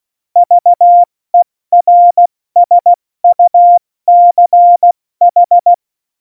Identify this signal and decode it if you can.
Morse code, heard as a keyed tone, about 16 wpm, 700 Hz. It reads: VERSUCH
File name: Morsekurs / Witze / Witz 08. Januar 2019, 70 F80